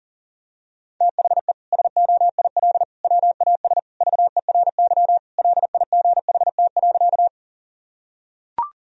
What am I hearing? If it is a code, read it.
Morse code, heard as a keyed tone, about 40 wpm, 700 Hz. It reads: THE SOIL WAS VERY LIGHT.
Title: Sentences from Top 1000 Words - Review Part 4 - 40wpm